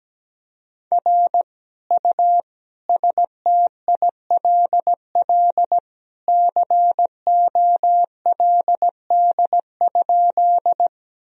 Morse code, heard as a keyed tone, about 17 wpm, 700 Hz. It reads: R U STILL COLD?
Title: Sentences from Top 400 Words and Abbreviations 17wpm